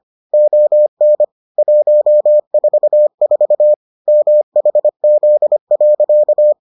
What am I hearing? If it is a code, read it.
Morse code, heard as a keyed tone, about 25 wpm, 600 Hz. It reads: ON 144 MHZ.